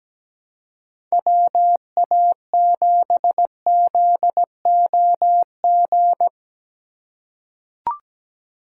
Morse code, heard as a keyed tone, about 17 wpm, 700 Hz. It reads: WA7ZOG
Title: USA Call Signs - 3x Character Spacing - 17wpm